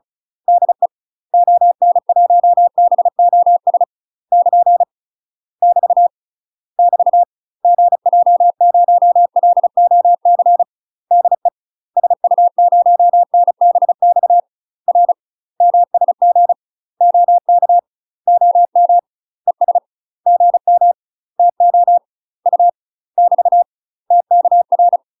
Morse code, heard as a keyed tone, about 36 wpm, 700 Hz. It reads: DE ON1BOS <KN> <BT> <BT> GJ0LOC DE SU0NBX R MSG OK OM ES GM TO U <BT> TKR
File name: UK MASTER QSOs TRAINING / 35WPM / UK MASTER QSOs 0035